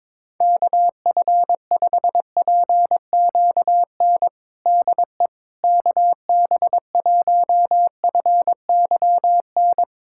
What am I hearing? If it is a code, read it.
Morse code, heard as a keyed tone, about 22 wpm, 700 Hz. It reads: KF5PQN DE KB1FYN
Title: USA Callsign DE Callsign - 6 Characters - 22wpm